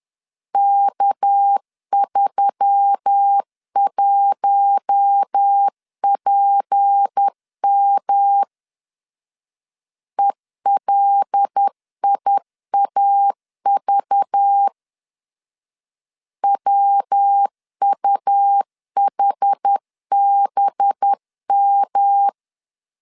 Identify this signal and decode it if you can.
Morse code, heard as a keyed tone, about 10 wpm, 800 Hz. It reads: K31PM ELIAV WUHBM
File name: LEZIONI / 6 PKX0 / 6 PKX0 10wpm 2707